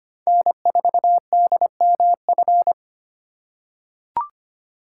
Morse code, heard as a keyed tone, about 25 wpm, 700 Hz. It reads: N4DMF
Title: USA Call Signs - 5 Characters - 4x Character Spacing - 25wpm